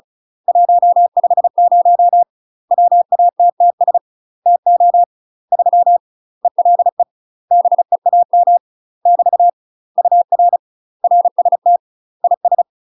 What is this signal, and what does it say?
Morse code, heard as a keyed tone, about 36 wpm, 700 Hz. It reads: 150 WATTS TO 3 ELE BEAM <BT> UR RST IS
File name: UK MASTER QSOs TRAINING / 35WPM / UK MASTER QSOs 0020